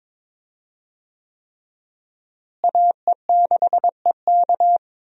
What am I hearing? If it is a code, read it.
Morse code, heard as a keyed tone, about 21 wpm, 700 Hz. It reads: AE6EK
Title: USA Call Signs - 5 Characters - Repeated 3x - 22wpm